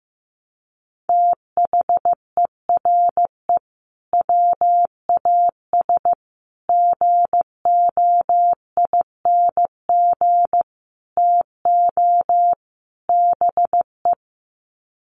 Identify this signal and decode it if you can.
Morse code, heard as a keyed tone, about 15 wpm, 700 Hz. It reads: THERE WAS GOING TO BE
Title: Sets of 5 Words Encore2 15wpm